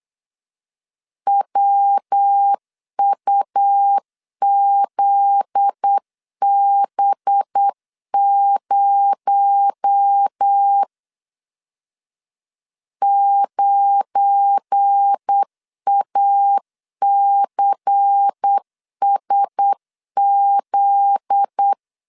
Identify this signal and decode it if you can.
Morse code, heard as a keyed tone, about 8 wpm, 800 Hz. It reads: WUZB0 9ACSZ
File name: LEZIONI / 7 GF9Z / 7 GF9Z 8wpm 3213